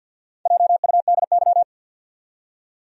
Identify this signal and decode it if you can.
Morse code, heard as a keyed tone, about 53 wpm, 700 Hz. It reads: JUDY
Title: USA Common Names - Max 4 Letters - 50wpm